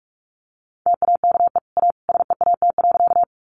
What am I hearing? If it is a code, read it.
Morse code, heard as a keyed tone, about 44 wpm, 700 Hz. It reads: TAKE A SEAT.
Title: Sentences from Top 1500 Words 45wpm